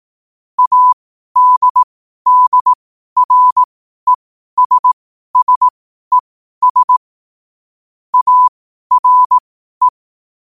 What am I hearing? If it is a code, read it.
Morse code, heard as a keyed tone, about 18 wpm, spaced as for 12 wpm, 1000 Hz. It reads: ADDRESSES ARE
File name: qotdfiles / 12wpm